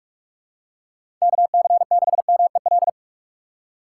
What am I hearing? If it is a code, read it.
Morse code, heard as a keyed tone, about 43 wpm, 700 Hz. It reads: KC6MEL
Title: USA Call Signs - 6 Characters - 3x Character Spacing - 45wpm